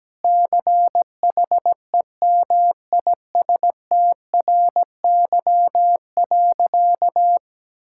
Morse code, heard as a keyed tone, about 17 wpm, 700 Hz. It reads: CHEMISTRY.
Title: Sentences from Top 1000 Words - Review Part 6 - 17wpm